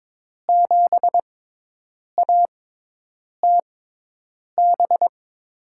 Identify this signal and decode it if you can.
Morse code, heard as a keyed tone, about 22 wpm, spaced as for 8 wpm, 700 Hz. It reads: 7ATB
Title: USA Call Signs - 5 Characters - 6x Character Spacing - 22wpm